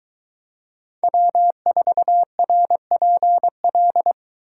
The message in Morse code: W4RPL

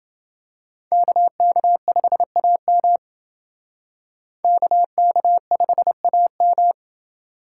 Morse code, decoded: KK5AM KK5AM

